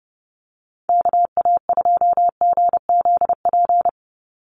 Morse code, heard as a keyed tone, about 30 wpm, 700 Hz. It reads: KA2GZP
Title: USA Call Signs 30wpm